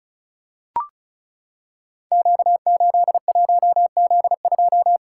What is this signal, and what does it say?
Morse code, heard as a keyed tone, about 36 wpm, 700 Hz. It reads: Q81Z2
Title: Code Groups - 5 Letter:Numbers - 35wpm